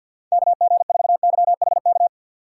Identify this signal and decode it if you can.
Morse code, heard as a keyed tone, about 50 wpm, 700 Hz. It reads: KG4YHK
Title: USA Call Signs - 2x Character Spacing - 50wpm